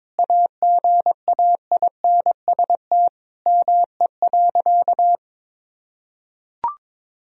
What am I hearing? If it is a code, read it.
Morse code, heard as a keyed tone, about 21 wpm, 700 Hz. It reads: AGAINST ME.
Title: Sentences from Top 1000 Words - Review Part 4 - 22wpm